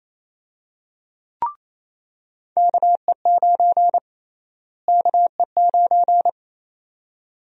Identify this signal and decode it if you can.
Morse code, heard as a keyed tone, about 28 wpm, 700 Hz. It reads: KE9 KE9
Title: USA Call Sign Prefixes - Repeated 2x - 28wpm